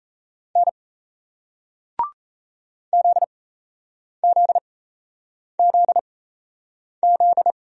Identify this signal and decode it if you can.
Morse code, heard as a keyed tone, about 38 wpm, spaced as for 8 wpm, 700 Hz. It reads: N ZZZZ